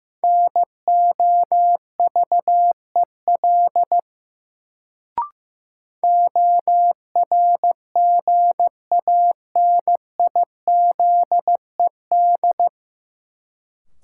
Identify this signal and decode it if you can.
Morse code, heard as a keyed tone, about 15 wpm, 700 Hz. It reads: NOVEL ORGANIZED
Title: Sentences from Top 1300 Words - New Word Drill Alt - 15wpm